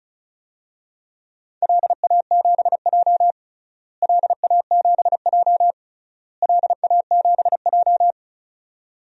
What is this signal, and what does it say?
Morse code, heard as a keyed tone, about 36 wpm, 700 Hz. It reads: LA7J LA7J LA7J
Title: DXCC-100 Call Signs - 4 Characters - Repeated 3x - 35wpm